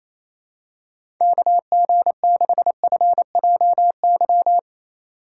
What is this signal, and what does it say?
Morse code, heard as a keyed tone, about 27 wpm, 700 Hz. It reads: KG6FJY
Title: USA Call Signs - 6 Characters - 4x Character Spacing - 28wpm